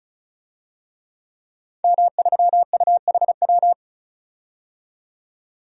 Morse code, heard as a keyed tone, about 36 wpm, 700 Hz. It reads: M3UHW
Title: DXCC-100 Call Signs - 5 Characters - 35wpm